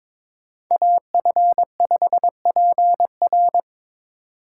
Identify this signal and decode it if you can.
Morse code, heard as a keyed tone, about 22 wpm, 700 Hz. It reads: AF5PR
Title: USA Call Signs - 5 Characters - 6x Character Spacing - 22wpm